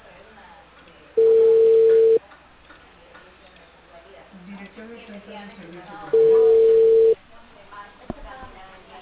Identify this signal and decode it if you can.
Ringback tone (China, 450 Hz), with speech.